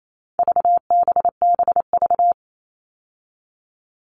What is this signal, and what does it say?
Morse code, heard as a keyed tone, about 28 wpm, 700 Hz. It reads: VBBV
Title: All Cut Numbers - 4-Digit 28wpm